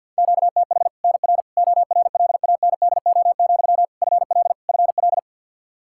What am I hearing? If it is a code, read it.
Morse code, heard as a keyed tone, about 51 wpm, 700 Hz. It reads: QTH NR ORLANDO, FL FL
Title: Practice QSO - Exchange 1 - Beginning - 50wpm